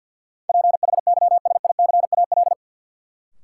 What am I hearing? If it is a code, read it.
Morse code, heard as a keyed tone, about 47 wpm, 700 Hz. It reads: PHYSICAL
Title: Sentences from Top 1000 Words - Review Part 4 - New Word Drill - 50wpm